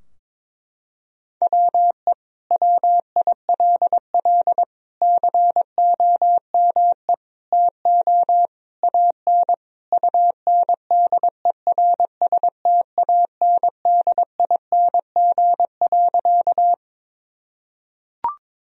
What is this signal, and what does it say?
Morse code, heard as a keyed tone, about 21 wpm, 700 Hz. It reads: WE WILL COME TO AN UNDERSTANDING.
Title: Sentences from Top 1000 Words - Review Part 3 - 22wpm